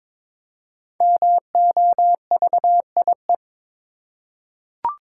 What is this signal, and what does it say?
Morse code, heard as a keyed tone, about 22 wpm, 700 Hz. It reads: MOVIE
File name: 5 Letter Words - First 3 Letters Given - 22wpm